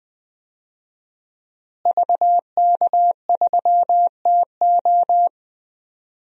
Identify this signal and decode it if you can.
Morse code, heard as a keyed tone, about 20 wpm, 700 Hz. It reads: VK3TO
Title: DXCC-100 Call Signs - 5 Characters - Repeated 3x - 20wpm